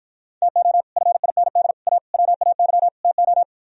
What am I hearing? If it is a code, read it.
Morse code, heard as a keyed tone, about 52 wpm, 700 Hz. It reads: TO FIND A WAY TO